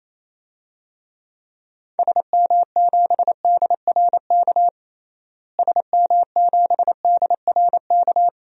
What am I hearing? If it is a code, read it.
Morse code, heard as a keyed tone, about 27 wpm, 700 Hz. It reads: SM7DRK SM7DRK
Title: DXCC-100 Call Signs - 6 Characters - Repeated 2x - 28wpm